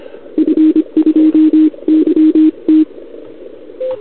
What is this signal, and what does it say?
Morse code, heard as a keyed tone, about 25 wpm, 310 Hz. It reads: F2YT
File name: CQWWCW2021 / Doderna OL3Z